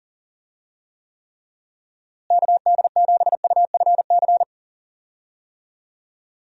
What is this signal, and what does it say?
Morse code, heard as a keyed tone, about 40 wpm, 700 Hz. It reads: KD7UFC
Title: USA Call Signs - 6 Characters - Repeated 3x - 40wpm